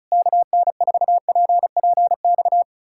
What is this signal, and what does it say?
Morse code, heard as a keyed tone, about 40 wpm, 700 Hz. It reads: KN4PPX